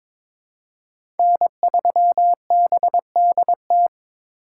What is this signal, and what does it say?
Morse code, heard as a keyed tone, about 21 wpm, 700 Hz. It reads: N3BDT
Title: USA Call Signs - 5 Characters - 2x Character Spacing - 22wpm